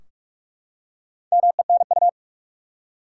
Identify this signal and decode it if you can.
Morse code, heard as a keyed tone, about 46 wpm, 700 Hz. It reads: MENU
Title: Four Letter Words - 45wpm